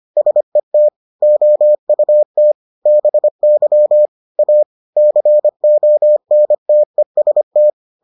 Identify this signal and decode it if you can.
Morse code, heard as a keyed tone, about 25 wpm, 600 Hz. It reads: SET OUT BY A CONTEST